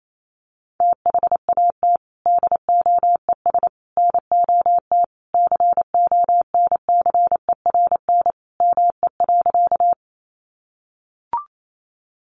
Morse code, heard as a keyed tone, about 28 wpm, 700 Hz. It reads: THAT DOES NOT CONCERN ME.
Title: Sentences from Top 1000 Words 28wpm